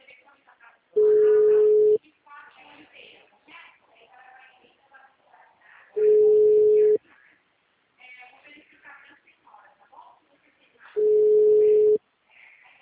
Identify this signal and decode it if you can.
Ringback tone (Europe, 425 Hz), with speech.